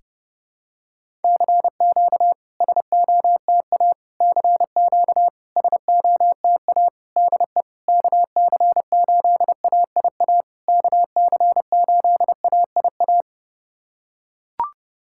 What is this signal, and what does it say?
Morse code, heard as a keyed tone, about 30 wpm, 700 Hz. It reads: CQ SOTA CQ SOTA DE KC8AIA KC8AIA
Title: SOTA QSO Practice 30wpm